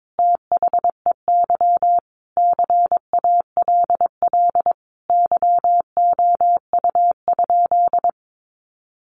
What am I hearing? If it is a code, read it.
Morse code, heard as a keyed tone, about 22 wpm, 700 Hz. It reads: THEY CALL YOU?
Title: Sentences from Top 100 Words 22wpm